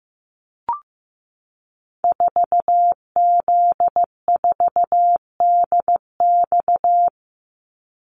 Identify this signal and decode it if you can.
Morse code, heard as a keyed tone, about 15 wpm, 700 Hz. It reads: 4Z4DX
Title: Call Signs Repeated 2x - 15wpm